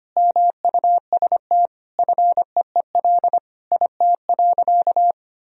Morse code, heard as a keyed tone, about 25 wpm, 700 Hz. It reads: MUST FEEL IT.
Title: Sentences from Top 400 Words and Abbreviations 25wpm